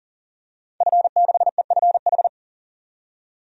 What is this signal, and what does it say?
Morse code, heard as a keyed tone, about 40 wpm, 700 Hz. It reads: F6EFH
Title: DXCC-100 Call Signs - 5 Characters - 40wpm